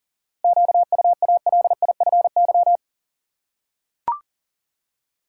Morse code, heard as a keyed tone, about 40 wpm, 700 Hz. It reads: QUALIFY
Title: Seven Letter Words - 40wpm